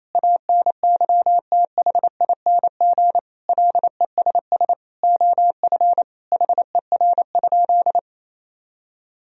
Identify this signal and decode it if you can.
Morse code, heard as a keyed tone, about 27 wpm, 700 Hz. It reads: ANYTHING LESS OF HER?
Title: Sentences from Top 1000 Words - Review Part 2 - 28wpm